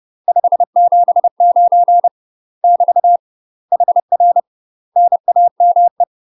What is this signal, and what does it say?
Morse code, heard as a keyed tone, about 30 wpm, 700 Hz. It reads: 579 <BT> HR NAME
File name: UK MASTER QSOs TRAINING/30WPM